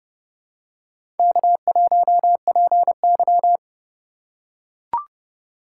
Morse code, heard as a keyed tone, about 32 wpm, 700 Hz. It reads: K1PY
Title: USA Call Signs - 4 Characters - 3x Character Spacing - 30wpm